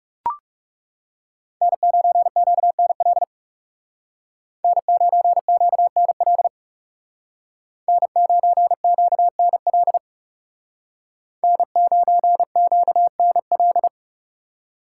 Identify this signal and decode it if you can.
Morse code, heard as a keyed tone, about 36 wpm, 700 Hz. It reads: N9QNL N9QNL N9QNL N9QNL
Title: Speed Racing - USA Call Signs 30wpm